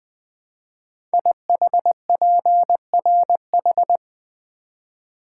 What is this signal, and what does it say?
Morse code, heard as a keyed tone, about 20 wpm, 700 Hz. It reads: IHPRH
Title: Code Groups - 5 Letters - 20wpm